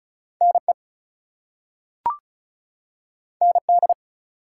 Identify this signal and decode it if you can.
Morse code, heard as a keyed tone, about 36 wpm, 700 Hz. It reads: NE ND
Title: US States and Canadian Provinces 35wpm